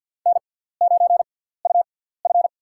Morse code, heard as a keyed tone, about 52 wpm, spaced as for 17 wpm, 700 Hz. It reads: N9UF
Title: USA Call Signs - 6x Character Spacing - 50wpm